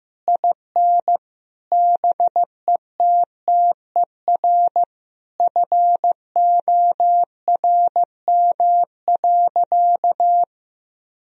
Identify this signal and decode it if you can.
Morse code, heard as a keyed tone, about 15 wpm, 700 Hz. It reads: IN BETTER FORM.
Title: Sentences from Top 1000 Words - Review Part 1 - 15wpm